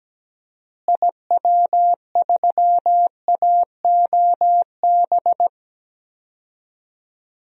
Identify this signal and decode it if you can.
Morse code, heard as a keyed tone, about 17 wpm, 700 Hz. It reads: IW3AOB
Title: DXCC-100 Call Signs - 6 Characters - Repeated 2x - 17wpm